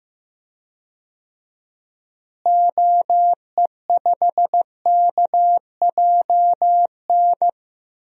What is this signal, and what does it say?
Morse code, heard as a keyed tone, about 15 wpm, 700 Hz. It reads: OE5KJN
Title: DXCC-100 Call Signs - 6 Characters - Repeated 3x - 15wpm